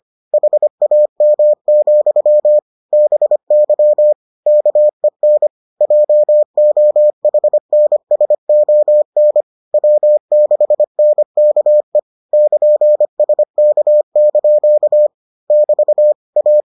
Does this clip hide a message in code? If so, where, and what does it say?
Yes: Morse code from 0.3 to 16.6 s, heard as a keyed tone, about 25 wpm, 600 Hz. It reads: HAM, BY KEN JOHNSON W6NKE <KN> SK) <BT> A